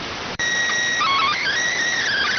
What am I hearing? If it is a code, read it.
SSTV (slow-scan television), PD 290.